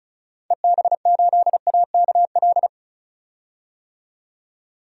Morse code, heard as a keyed tone, about 36 wpm, 700 Hz. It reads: EB8AKL
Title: DXCC-100 Call Signs - 6 Characters - 35wpm